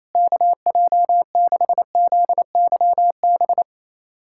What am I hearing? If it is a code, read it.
Morse code, heard as a keyed tone, about 28 wpm, 700 Hz. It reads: KJ6ZYB
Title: USA Callsign DE Callsign - 6 Characters - 28wpm